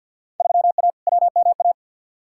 Morse code, heard as a keyed tone, about 46 wpm, 700 Hz. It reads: 3A WMA
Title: Field Day Exchanges Encore - 50wpm